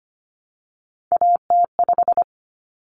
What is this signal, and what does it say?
Morse code, heard as a keyed tone, about 25 wpm, 700 Hz. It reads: AT5
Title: Common Cut Numbers - 3-Digit 25wpm